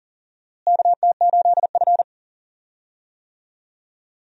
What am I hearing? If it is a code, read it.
Morse code, heard as a keyed tone, about 40 wpm, 700 Hz. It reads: KT8F
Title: USA Call Signs - 4 Characters - Repeated 3x - 40wpm